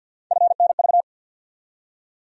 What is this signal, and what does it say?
Morse code, heard as a keyed tone, about 54 wpm, 700 Hz. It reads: FNV
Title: USA Call Sign Postfixes - 3 Letters 50wpm